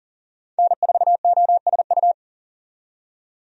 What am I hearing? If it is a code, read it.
Morse code, heard as a keyed tone, about 40 wpm, 700 Hz. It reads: N4OSU
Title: USA Call Signs - 5 Characters - 40wpm